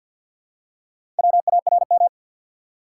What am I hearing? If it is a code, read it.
Morse code, heard as a keyed tone, about 51 wpm, 700 Hz. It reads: WARM
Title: Four Letter Words - 50wpm